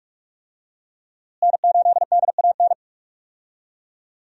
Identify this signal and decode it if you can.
Morse code, heard as a keyed tone, about 47 wpm, 700 Hz. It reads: N8DAN
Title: USA Call Signs - 5 Characters - Repeated 2x - 45wpm